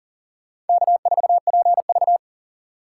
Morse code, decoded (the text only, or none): K4PV